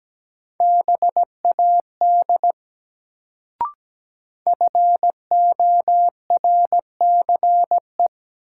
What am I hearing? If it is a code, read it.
Morse code, heard as a keyed tone, about 17 wpm, 700 Hz. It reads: BAD FORCE
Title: Sentences from Top 700 Words - New Word Drill - 17wpm